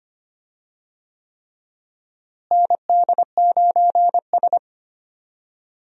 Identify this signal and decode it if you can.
Morse code, heard as a keyed tone, about 25 wpm, 700 Hz. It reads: ND9S
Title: USA Call Signs - 4 Characters - Repeated 3x - 25wpm